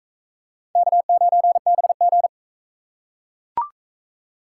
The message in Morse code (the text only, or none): K9BG